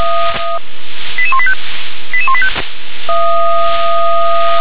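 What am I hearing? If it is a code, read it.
Five-tone selective call (ZVEI), heard as quick runs of five tones, tones of about 70 ms. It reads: 88185 88185
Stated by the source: ZVEI five-tone selective call received by SDR.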